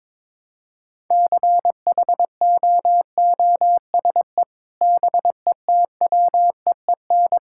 Morse code, heard as a keyed tone, about 22 wpm, 700 Hz. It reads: CHOOSE BETWEEN
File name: practice / 22wpm / Sentences with 3-Word Phrases - 22wpm